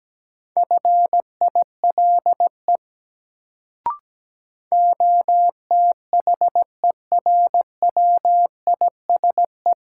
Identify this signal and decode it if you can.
Morse code, heard as a keyed tone, about 17 wpm, 700 Hz. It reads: FILE OTHERWISE
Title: Sentences from Top 1200 Words - New Word Drill Alt - 17wpm